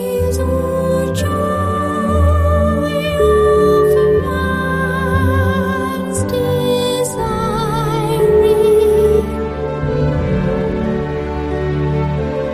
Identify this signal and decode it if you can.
Ringback tone (Europe, 425 Hz), with noise.